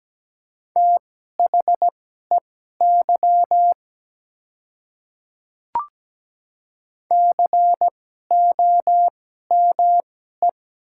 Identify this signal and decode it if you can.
Morse code, heard as a keyed tone, about 17 wpm, spaced as for 12 wpm, 700 Hz. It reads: THEY COME
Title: 2x Character Spacing - Top 100 Words - 17wpm